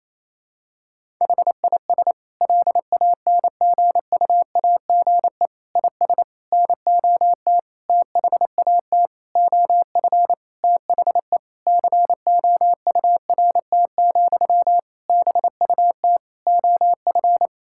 Morse code, heard as a keyed tone, about 28 wpm, 700 Hz. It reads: HIS LANGUAGE IS NOT THAT OF THE COURT, BUT OF
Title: Sentences from Top 1600 Words 28wpm